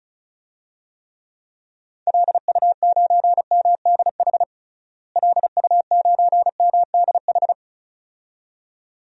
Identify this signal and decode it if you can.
Morse code, heard as a keyed tone, about 34 wpm, 700 Hz. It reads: LU9MDH LU9MDH
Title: DXCC-100 Call Signs - 6 Characters - Repeated 2x - 35wpm